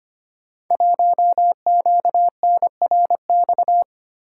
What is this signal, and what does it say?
Morse code, heard as a keyed tone, about 25 wpm, 700 Hz. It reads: 1QNRX
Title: Code Groups - 5 Letter:Numbers - 25wpm